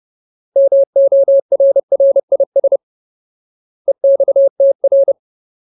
Morse code, heard as a keyed tone, about 30 wpm, 550 Hz. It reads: MORRIS EXTR
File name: Ultimate Word List / 30WPM Ultimate